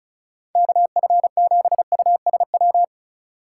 Morse code, heard as a keyed tone, about 34 wpm, 700 Hz. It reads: KF7USW
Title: USA Call Signs - 6 Characters - 3x Character Spacing - 35wpm